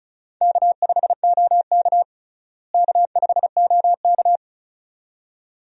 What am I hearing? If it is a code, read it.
Morse code, heard as a keyed tone, about 34 wpm, 700 Hz. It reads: K5OK K5OK
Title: USA Call Signs - 4 Characters - Repeated 2x - 35wpm